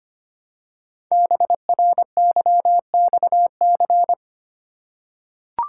Morse code, heard as a keyed tone, about 25 wpm, 700 Hz. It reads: BRYXC
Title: Code Groups - 5 Letters - 25wpm